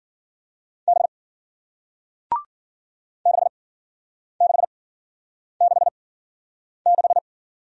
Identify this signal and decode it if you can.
Morse code, heard as a keyed tone, about 53 wpm, spaced as for 9 wpm, 700 Hz. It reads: B 6666